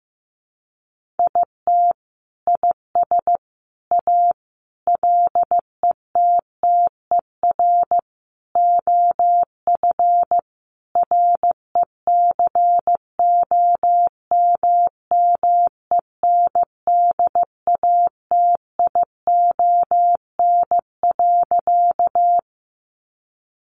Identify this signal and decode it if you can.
Morse code, heard as a keyed tone, about 15 wpm, 700 Hz. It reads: IT IS A LETTER OF RECOMMENDATION.
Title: Sentences from Top 1400 Words 15wpm